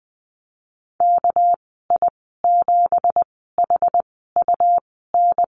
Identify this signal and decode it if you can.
Morse code, heard as a keyed tone, about 20 wpm, spaced as for 14 wpm, 700 Hz. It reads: KI7HUN
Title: 213 Lesson 34g - Number 8 - Call Signs 20wpm